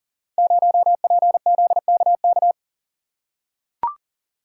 Morse code, heard as a keyed tone, about 42 wpm, 700 Hz. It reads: 0PZKK
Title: Code Groups - 5 Letter:Numbers - 40wpm